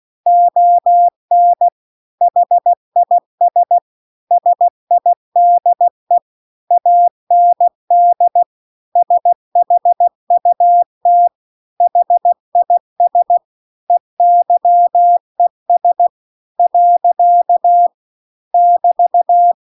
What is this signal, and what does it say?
Morse code, heard as a keyed tone, about 16 wpm, 700 Hz. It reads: ON HIS SIDE AND SHUT HIS EYES . <BT>